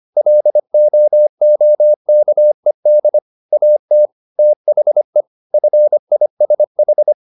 Morse code, heard as a keyed tone, about 25 wpm, 600 Hz. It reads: LOOKED AT THE FISH